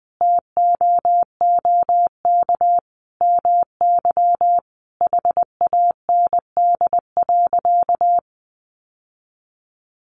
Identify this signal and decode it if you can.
Morse code, heard as a keyed tone, about 19 wpm, 700 Hz. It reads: TOOK MY HAND.
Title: Sentences from Top 1100 Words 20wpm